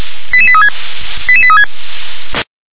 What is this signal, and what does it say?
Five-tone selective call (ZVEI), heard as quick runs of five tones, tones of about 70 ms. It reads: 88836 88836
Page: /357/SDR/ZVEI/